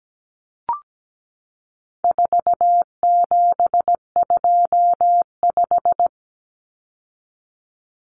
Morse code, heard as a keyed tone, about 17 wpm, 700 Hz. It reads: 4725
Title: Numbers - 4-Digit 17wpm